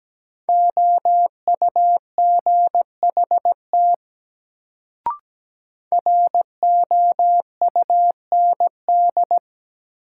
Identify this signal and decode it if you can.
Morse code, heard as a keyed tone, about 17 wpm, 700 Hz. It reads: OUGHT ROUND
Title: Sets of 5 Words Encore - New Word Drill - 17wpm